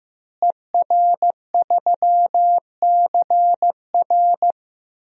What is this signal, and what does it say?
Morse code, heard as a keyed tone, about 15 wpm, 700 Hz. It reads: ER3CR
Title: DXCC-100 Call Signs - 5 Characters - Repeated 2x - 15wpm